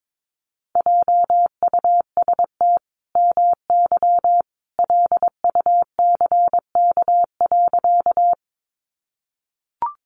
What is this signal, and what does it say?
Morse code, heard as a keyed tone, about 22 wpm, 700 Hz. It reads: JUST MY LUCK.
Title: Sentences from Top 1500 Words 22wpm